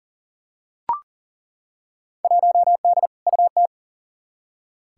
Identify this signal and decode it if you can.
Morse code, heard as a keyed tone, about 40 wpm, 700 Hz. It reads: 1D UT
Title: Field Day Exchanges - 40wpm